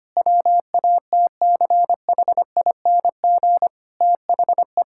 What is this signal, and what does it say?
Morse code, heard as a keyed tone, about 25 wpm, 700 Hz. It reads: WATCHING THE
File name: practice / 25wpm / Difficult Sentences with Hints - 25wpm